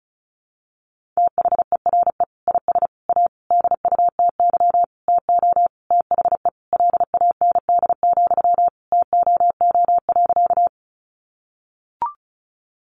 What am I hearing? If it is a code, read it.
Morse code, heard as a keyed tone, about 34 wpm, 700 Hz. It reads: THERE IS A DUTY TO THE LAND, TOO.
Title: Sentences from Top 1500 Words 35wpm